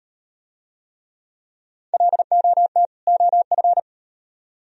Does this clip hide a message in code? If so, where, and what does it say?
Yes: Morse code from 1.9 to 3.8 s, heard as a keyed tone, about 38 wpm, 700 Hz. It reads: LOT OF